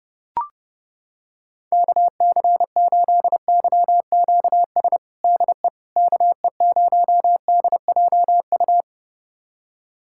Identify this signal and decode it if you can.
Morse code, heard as a keyed tone, about 31 wpm, 700 Hz. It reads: KC8YQS DE KE0DJU
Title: USA Callsign DE Callsign 30wpm